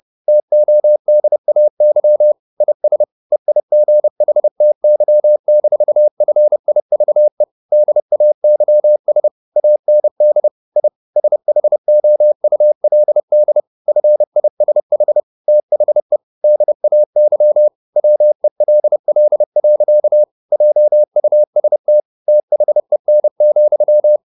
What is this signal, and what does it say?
Morse code, heard as a keyed tone, about 30 wpm, 600 Hz. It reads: TODAY IS EIGHTY-FIVE DAYS AND I SHOULD FISH THE DAY WELL. JUST THEN,
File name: the old man and the sea / the old man and the sea 1 30wpm